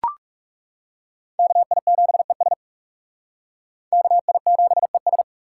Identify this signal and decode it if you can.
Morse code, heard as a keyed tone, about 40 wpm, 700 Hz. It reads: KI7ES KI7ES